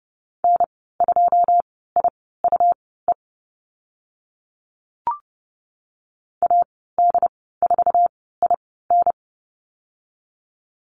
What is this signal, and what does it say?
Morse code, heard as a keyed tone, about 30 wpm, spaced as for 17 wpm, 700 Hz. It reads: N2IUE AD4IN
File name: Lesson 19g - Letter w - Call Signs 30wpm